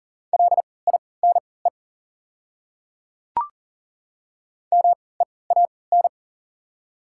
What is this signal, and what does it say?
Morse code, heard as a keyed tone, about 40 wpm, spaced as for 23 wpm, 690 Hz. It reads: LINE MEAN
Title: 3x Character Spacing - Top 200 Words - 40wpm